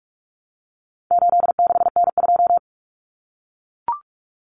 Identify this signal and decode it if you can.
Morse code, heard as a keyed tone, about 44 wpm, 700 Hz. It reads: 86N2
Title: Common Cut Numbers - 4-Digit 45wpm